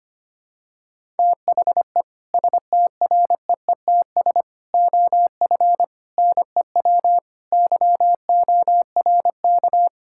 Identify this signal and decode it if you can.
Morse code, heard as a keyed tone, about 25 wpm, 700 Hz. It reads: THE STREETS OF NEW YORK
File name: practice / 25wpm / Sentences with 3-Word Phrases - 25wpm